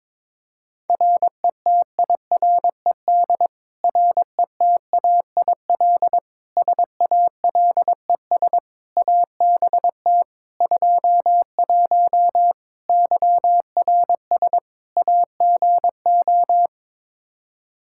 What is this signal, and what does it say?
Morse code, heard as a keyed tone, about 22 wpm, 700 Hz. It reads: RETIRED RETAIL SALES ABT 21 YRS AGO
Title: Practice QSO - Exchange 3 - More Info - 22wpm